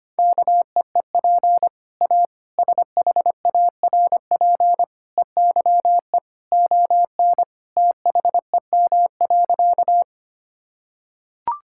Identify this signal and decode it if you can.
Morse code, heard as a keyed tone, about 25 wpm, 700 Hz. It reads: KEEP A SHARP EYE ON THEM.
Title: Sentences from Top 1600 Words 25wpm